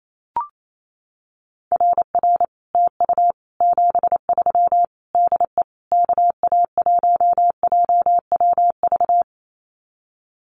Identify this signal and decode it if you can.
Morse code, heard as a keyed tone, about 28 wpm, 700 Hz. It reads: RR TU 73 DE KA1JWV
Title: Common QSO Phrases Encore 28wpm